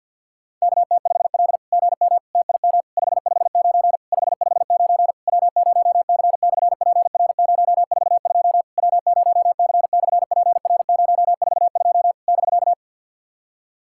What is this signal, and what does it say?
Morse code, heard as a keyed tone, about 48 wpm, 690 Hz. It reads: KT5L GM TIM 559 559 W0C/PR042 W0C/PR042 <BK>
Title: SOTA QSO Practice 50wpm